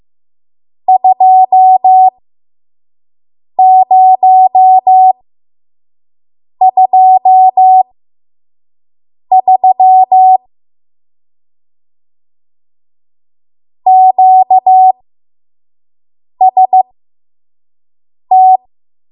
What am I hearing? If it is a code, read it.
Morse code, heard as a keyed tone, about 15 wpm, spaced as for 5 wpm, 750 Hz. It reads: 2023 QST
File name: Morse / 05WPM